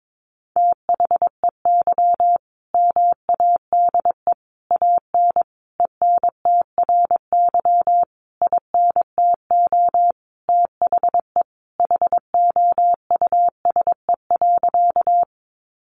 Morse code, heard as a keyed tone, about 22 wpm, 700 Hz. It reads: THEY MADE AN ENTRY INTO THE HOUSE.
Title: Sentences from Top 1500 Words 22wpm